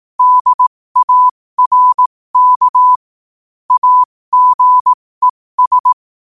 Morse code, heard as a keyed tone, about 18 wpm, spaced as for 15 wpm, 1000 Hz. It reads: DARK AGES
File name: qotdfiles / 15wpm